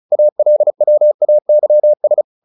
Morse code, heard as a keyed tone, about 34 wpm, 600 Hz. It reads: ALWAYS
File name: the old man and the sea / the old man and the sea 1 35wpm